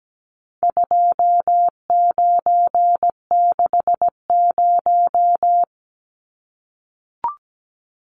Morse code, heard as a keyed tone, about 17 wpm, 700 Hz. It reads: 2960
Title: Numbers - 4-Digit 17wpm